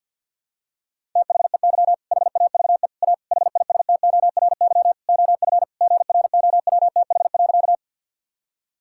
Morse code, heard as a keyed tone, about 51 wpm, 690 Hz. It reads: THEY HAVE A HISTORY OF GROWTH.
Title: Sentences from Top 1000 Words - Review Part 5 - 50wpm